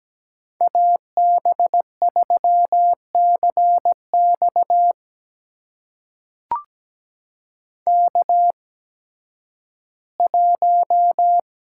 Morse code, heard as a keyed tone, about 17 wpm, spaced as for 11 wpm, 700 Hz. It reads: AB3CX K1
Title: USA Call Signs - 5 Characters - 8x Character Spacing - 17wpm